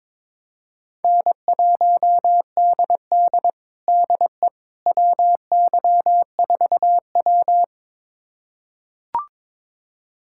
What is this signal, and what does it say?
Morse code, heard as a keyed tone, about 22 wpm, 700 Hz. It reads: N1DD DE WY4W
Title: USA Callsign DE Callsign - 4 Characters - 22wpm